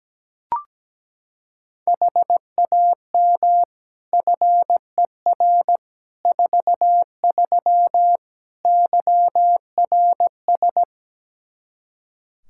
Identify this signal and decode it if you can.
Morse code, heard as a keyed tone, about 17 wpm, 700 Hz. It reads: HAM FER 43 YRS
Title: Practice QSO - Exchange 3 - More Info - 17wpm